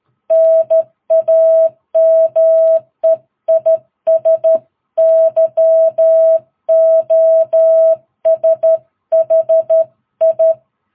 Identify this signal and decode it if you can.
Morse code, heard as a keyed tone, about 13 wpm, 660 Hz. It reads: NAME IS YOSHI
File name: jh8cbh / 2022cw